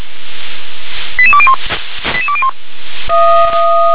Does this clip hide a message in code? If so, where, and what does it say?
Yes: five-tone selective call (ZVEI) from 1.2 to 2.5 s, heard as quick runs of five tones, tones of about 70 ms. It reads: 88201 88201